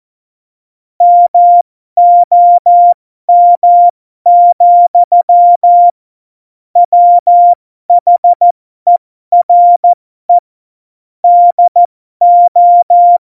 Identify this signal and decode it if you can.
Morse code, heard as a keyed tone, about 14 wpm, spaced as for 12 wpm, 700 Hz. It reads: MOM, WHERE DO
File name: Morsekurs / Witze